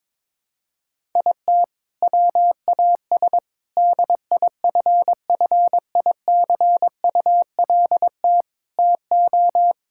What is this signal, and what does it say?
Morse code, heard as a keyed tone, about 21 wpm, 700 Hz. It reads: IT WAS DIFFICULT TO
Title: Sentences from Top 1000 Words - Review Part 5 - 22wpm